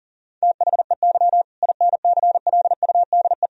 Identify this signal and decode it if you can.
Morse code, heard as a keyed tone, about 40 wpm, 700 Hz. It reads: THEY INCLUDE
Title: Sentences from Top 1000 Words - Review Part 3 - 40wpm